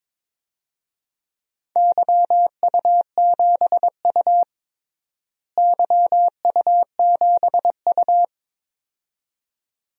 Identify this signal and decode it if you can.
Morse code, heard as a keyed tone, about 22 wpm, 700 Hz. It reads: YU7U YU7U
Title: DXCC-100 Call Signs - 4 Characters - Repeated 2x - 22wpm